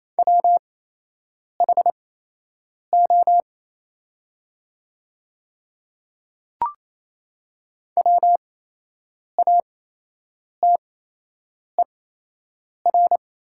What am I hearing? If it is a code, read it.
Morse code, heard as a keyed tone, about 28 wpm, spaced as for 8 wpm, 700 Hz. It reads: WHO WATER
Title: 8x Character Spacing - Top 100 Words - 28wpm